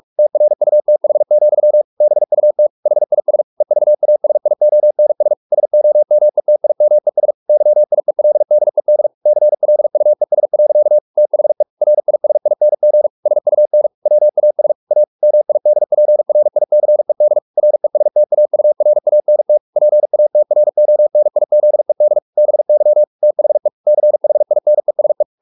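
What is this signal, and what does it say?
Morse code, heard as a keyed tone, about 44 wpm, 600 Hz. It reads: TRUTH, BUT HIS EVASIONS SOMETIMES YIELDED CLUES. THE RISING SUN WAS A MIDPRICED RESTAURANT PATRONIZED BY THE CHINESE
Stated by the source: Morse practice at 45 wpm, the novel Never by Ken Follett in English, from CW med Gnister.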